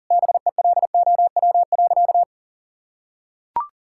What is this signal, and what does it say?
Morse code, heard as a keyed tone, about 40 wpm, 700 Hz. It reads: BELOW.
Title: Sentences from Top 1000 Words - Review Part 6 - 40wpm